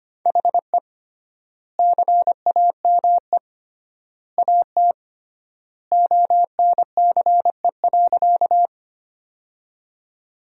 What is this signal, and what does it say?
Morse code, heard as a keyed tone, about 25 wpm, 700 Hz. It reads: HE CAME AT ONCE.
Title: 3x Word-Spacing - Sentences from Top 300 Words - 25wpm